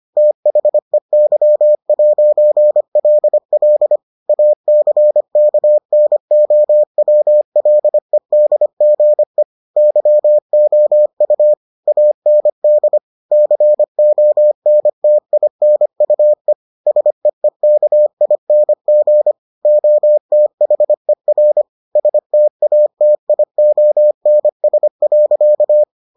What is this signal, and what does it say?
Morse code, heard as a keyed tone, about 25 wpm, 600 Hz. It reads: THEY'LL ACKNOWLEDGE YOU AND CONTINUE SEEKING OTHER STATIONS.